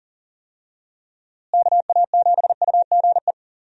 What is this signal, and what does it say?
Morse code, heard as a keyed tone, about 37 wpm, 700 Hz. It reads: KA7UGE
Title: USA Call Signs - 6 Characters - Repeated 3x - 40wpm